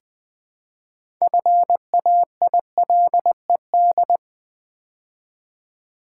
Morse code, heard as a keyed tone, about 20 wpm, 700 Hz. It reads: FAILED
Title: Build A Word - 3 Letters At A Time - 6-Letter Words - 20wpm